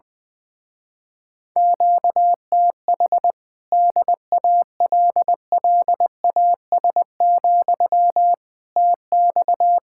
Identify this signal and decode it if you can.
Morse code, heard as a keyed tone, about 20 wpm, 700 Hz. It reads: QTH DALLAS, TX
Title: Speed-Racing Common QSO Phrases - 20wpm